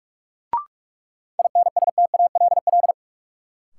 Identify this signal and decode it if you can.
Morse code, heard as a keyed tone, about 42 wpm, 700 Hz. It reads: INSTALL
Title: Sentences from Top 1600 Words - New Word Drill - 45wpm